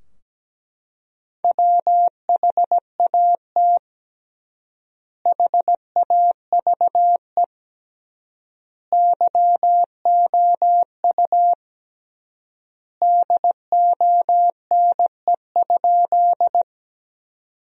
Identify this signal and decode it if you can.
Morse code, heard as a keyed tone, about 17 wpm, 700 Hz. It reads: WHAT HAVE YOU DONE?
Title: 3x Word-Spacing - Sentences from Top 500 Words - 17wpm